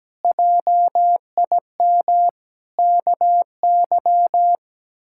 Morse code, heard as a keyed tone, about 17 wpm, 700 Hz. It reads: JIM KY
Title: K1USN SST QSO Practice - 17wpm